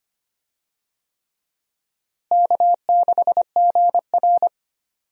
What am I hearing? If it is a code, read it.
Morse code, heard as a keyed tone, about 25 wpm, 700 Hz. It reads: K6GR
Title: USA Call Signs - 4 Characters - Repeated 3x - 25wpm